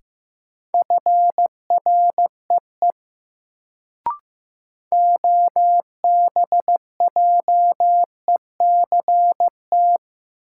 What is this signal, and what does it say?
Morse code, heard as a keyed tone, about 15 wpm, 700 Hz. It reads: FREE OBJECT
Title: Sentences from Top 500 Words - New Word Drill - 15wpm